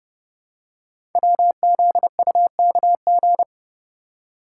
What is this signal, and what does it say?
Morse code, heard as a keyed tone, about 30 wpm, 700 Hz. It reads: WZUKG
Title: Code Groups - 5 Letters - 30wpm